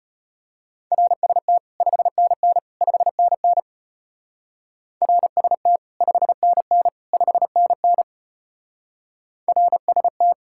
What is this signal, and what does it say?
Morse code, heard as a keyed tone, about 30 wpm, 700 Hz. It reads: RST 5NN 5NN RST 5NN 5NN RST
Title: Speed-Racing Common QSO Phrases - 25wpm